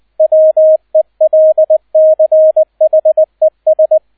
Morse code, heard as a keyed tone, about 20 wpm, 630 Hz, with radio noise. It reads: WELCHES